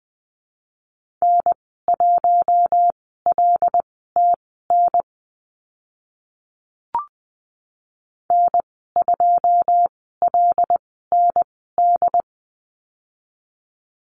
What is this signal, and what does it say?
Morse code, heard as a keyed tone, about 20 wpm, spaced as for 14 wpm, 700 Hz. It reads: N1LTN N2LND
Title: 082 Lesson 14g - Number 2 - Call Signs 20wpm